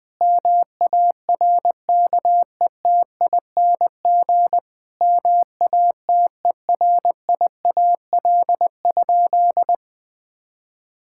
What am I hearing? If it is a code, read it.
Morse code, heard as a keyed tone, about 21 wpm, 700 Hz. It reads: MARKETING MATERIAL?
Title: Sentences from Top 1000 Words - Review Part 5 - 20wpm